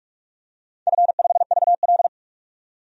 Morse code, heard as a keyed tone, about 46 wpm, 700 Hz. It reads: F5VL
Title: DXCC-100 Call Signs - 4 Characters - 45wpm